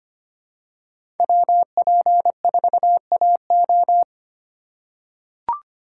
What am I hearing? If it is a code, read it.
Morse code, heard as a keyed tone, about 25 wpm, 700 Hz. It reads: WP4AO
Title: USA Call Signs - 8x Character Spacing - 25wpm